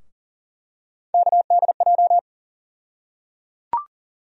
Morse code, heard as a keyed tone, about 37 wpm, 700 Hz. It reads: KDJ
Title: USA Call Sign Postfixes - 3 Letters 40wpm